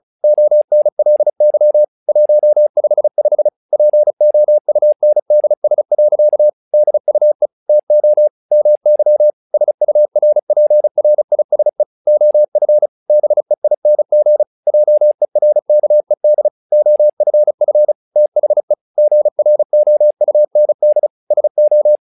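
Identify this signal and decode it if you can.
Morse code, heard as a keyed tone, about 34 wpm, 600 Hz. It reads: ONLY 155 POUNDS. DUE TO MY SURPRISE OF BEING JERKED OFF THE GROUND SO
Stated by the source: Morse practice text "Tower Trouble" sent at 35 WPM, from the SKCC - Straight Key Century Club learning center audio files.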